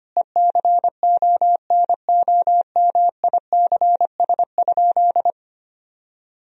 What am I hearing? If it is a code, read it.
Morse code, heard as a keyed tone, about 25 wpm, 700 Hz. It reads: ECONOMICS?
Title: Sentences from Top 1000 Words - Review Part 3 - 25wpm